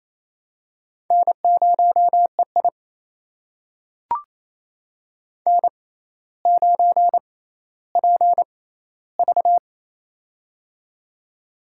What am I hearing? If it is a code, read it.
Morse code, heard as a keyed tone, about 28 wpm, spaced as for 14 wpm, 700 Hz. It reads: N0EI N9PV
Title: USA Call Signs - 4 Characters - 6x Character Spacing - 28wpm